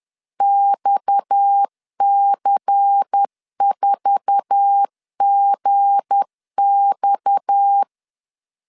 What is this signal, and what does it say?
Morse code, heard as a keyed tone, about 10 wpm, 800 Hz. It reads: XC4GX
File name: CW / DET / 1 9 1500C S A 10wpm 4511